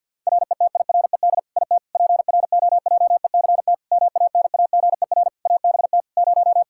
Morse code, heard as a keyed tone, about 50 wpm, 700 Hz. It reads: RETIRED IT PROJECT MANAGER ABT 0
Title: Practice QSO - Exchange 3 - More Info - 50wpm